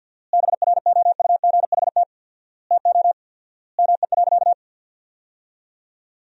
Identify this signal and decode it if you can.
Morse code, heard as a keyed tone, about 48 wpm, 700 Hz. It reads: BROUGHT TO ME.
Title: 4x Word-Spacing - Sentences from Top 500 Words - 50wpm